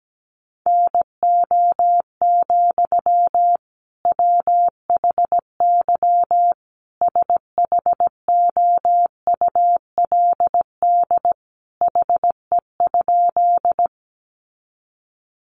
Morse code, heard as a keyed tone, about 17 wpm, 700 Hz. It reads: NO, WHY SHOULD HE?
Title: Sentences from Top 300 Words 17wpm